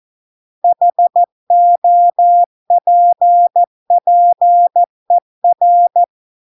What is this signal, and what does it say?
Morse code, heard as a keyed tone, about 14 wpm, 700 Hz. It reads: HOPPER